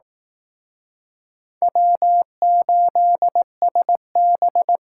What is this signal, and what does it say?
Morse code, heard as a keyed tone, about 19 wpm, 700 Hz. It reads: W8SB
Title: Speed Racing - USA 1x2 Call Signs 15wpm